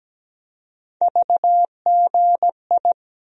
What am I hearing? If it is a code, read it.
Morse code, heard as a keyed tone, about 17 wpm, 700 Hz. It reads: VGI
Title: USA Call Sign Postfixes - 3 Letters 17wpm